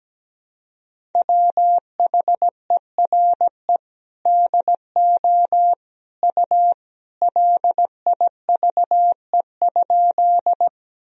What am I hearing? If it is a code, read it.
Morse code, heard as a keyed tone, about 17 wpm, 700 Hz. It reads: WHERE DO U LIVE?
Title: Sentences from Top 200 Words and Abbreviations 17wpm